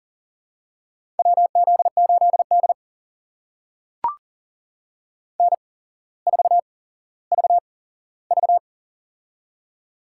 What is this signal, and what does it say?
Morse code, heard as a keyed tone, about 40 wpm, spaced as for 17 wpm, 700 Hz. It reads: WZ8D N4VV
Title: USA Call Signs - 4 Characters - 8x Character Spacing - 40wpm